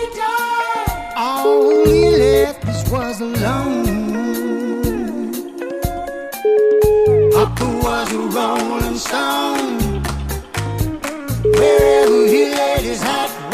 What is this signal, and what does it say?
Ringback tone (Europe, 425 Hz), with noise.